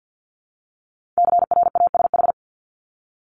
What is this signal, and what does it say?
Morse code, heard as a keyed tone, about 50 wpm, 700 Hz. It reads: CRASH
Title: Multi-letter Phonemes Repeated 3x 50wpm